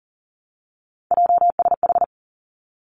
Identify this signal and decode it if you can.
Morse code, heard as a keyed tone, about 42 wpm, 700 Hz. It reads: JSH
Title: Three Challenging Letter-Number Combinations 40wpm